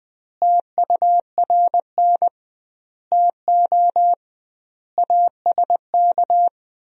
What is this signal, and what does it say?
Morse code, heard as a keyed tone, about 20 wpm, 700 Hz. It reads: TURN TO ASK
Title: 2x Word-Spacing - Sentences from Top 200 Words - 20wpm